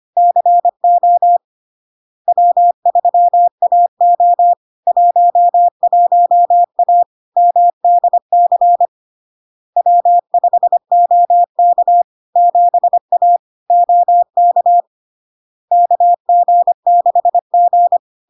Morse code, heard as a keyed tone, about 25 wpm, 700 Hz. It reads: CO W3AO 11A MDC W5OK 7A OK KG6G
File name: Log-in-CW-25wpm-700-Hz